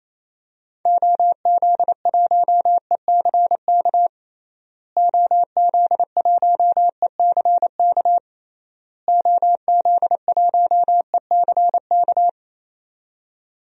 Morse code, heard as a keyed tone, about 28 wpm, 700 Hz. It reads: OZ1ECK OZ1ECK OZ1ECK
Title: DXCC-100 Call Signs - 6 Characters - Repeated 3x - 28wpm